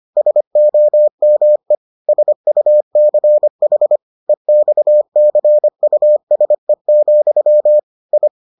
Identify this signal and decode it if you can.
Morse code, heard as a keyed tone, about 25 wpm, 600 Hz. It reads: SOME SUCH EXCUSE, I